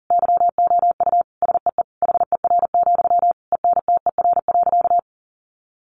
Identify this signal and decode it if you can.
Morse code, heard as a keyed tone, about 40 wpm, 700 Hz. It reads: YOU SEE HER, ENTER.
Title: Sentences from Top 1300 Words 40wpm